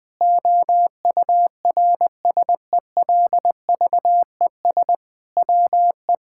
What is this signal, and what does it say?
Morse code, heard as a keyed tone, about 20 wpm, 700 Hz. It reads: OURSELVES WE
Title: Sentences from Top 1600 Words 20wpm